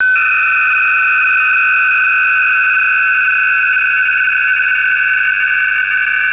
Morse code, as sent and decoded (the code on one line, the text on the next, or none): none